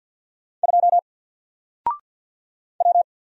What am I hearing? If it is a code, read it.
Morse code, heard as a keyed tone, about 49 wpm, 700 Hz. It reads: 2 W
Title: Single Letter-Number 50wpm